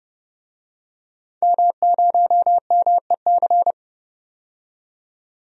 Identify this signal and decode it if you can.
Morse code, heard as a keyed tone, about 30 wpm, 700 Hz. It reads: M0MEC
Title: DXCC-100 Call Signs - 5 Characters - 30wpm